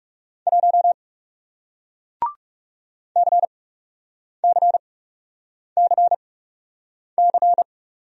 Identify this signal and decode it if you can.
Morse code, heard as a keyed tone, about 40 wpm, spaced as for 8 wpm, 700 Hz. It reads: 1 CCCC